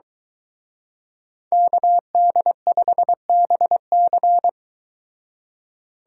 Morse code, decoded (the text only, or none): KD5BC